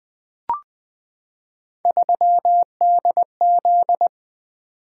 Morse code, heard as a keyed tone, about 20 wpm, 700 Hz. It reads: 3DZ
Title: Three Letter-Number Combinations - 20wpm